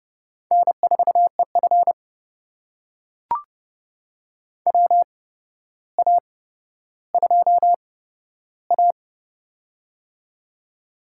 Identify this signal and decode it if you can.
Morse code, heard as a keyed tone, about 32 wpm, spaced as for 13 wpm, 700 Hz. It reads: N4EF WA2A
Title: USA Call Signs - 4 Characters - 8x Character Spacing - 30wpm